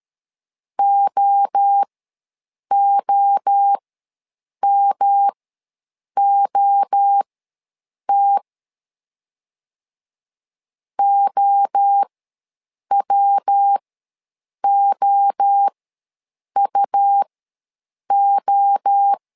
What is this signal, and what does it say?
Morse code, heard as a keyed tone, about 13 wpm, spaced as for 7 wpm, 800 Hz. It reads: OOMOT OWOUO
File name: LEZIONI / 3 MWRO / 3 MWRO ws 12wpm 1914